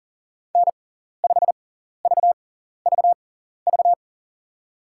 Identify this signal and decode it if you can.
Morse code, heard as a keyed tone, about 40 wpm, spaced as for 14 wpm, 700 Hz. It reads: N5VVV
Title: USA Call Signs - 5 Characters - 6x Character Spacing - 40wpm